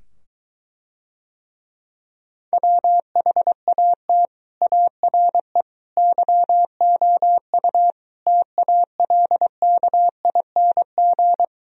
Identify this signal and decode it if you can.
Morse code, heard as a keyed tone, about 23 wpm, 700 Hz. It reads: WHAT ARE YOU TALKING